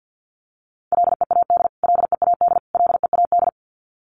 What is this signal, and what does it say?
Morse code, heard as a keyed tone, about 48 wpm, 700 Hz. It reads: LEAD LEAD LEAD
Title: Top 500 Words Repeated 50wpm